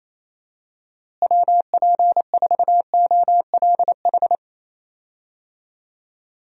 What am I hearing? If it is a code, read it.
Morse code, heard as a keyed tone, about 28 wpm, 700 Hz. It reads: WP4OLH
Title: USA Call Signs - 6 Characters - Repeated 3x - 28wpm